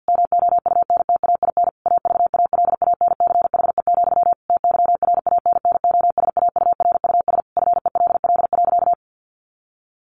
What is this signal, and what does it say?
Morse code, heard as a keyed tone, about 50 wpm, 700 Hz. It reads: MOUNTAIN AVALANCHE, TYRANNOSAURUS FELL.
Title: A Sound Of Thunder 50wpm